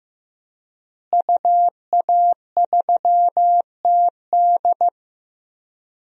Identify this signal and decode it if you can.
Morse code, heard as a keyed tone, about 15 wpm, 700 Hz. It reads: UA3TD
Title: DXCC-100 Call Signs - 5 Characters - Repeated 3x - 15wpm